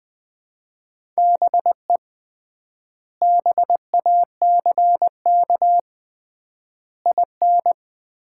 Morse code, heard as a keyed tone, about 20 wpm, 700 Hz. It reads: BE BACK IN
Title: 3x Word-Spacing - Sets of 5 Words - 20wpm